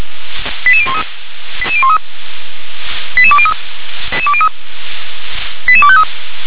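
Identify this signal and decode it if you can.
Five-tone selective call (ZVEI), heard as quick runs of five tones, tones of about 70 ms. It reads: 88013 88203 88253
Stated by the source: Received on an SDR as a ZVEI five-tone call.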